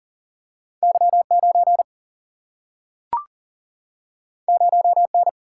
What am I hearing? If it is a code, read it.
Morse code, heard as a keyed tone, about 40 wpm, 700 Hz. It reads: Y 9 0 N
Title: Single-Letter Single-Number 40wpm